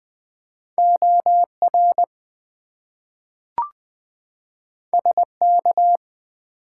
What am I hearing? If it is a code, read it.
Morse code, heard as a keyed tone, about 20 wpm, 700 Hz. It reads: OR SK
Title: US States and Canadian Provinces 20wpm